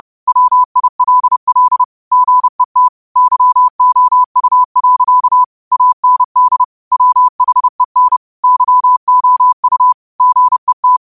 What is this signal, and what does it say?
Morse code, heard as a keyed tone, about 30 wpm, 1000 Hz. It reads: WILL GET YOU. AND WHEN YOU GET
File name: qotdfiles / 30wpm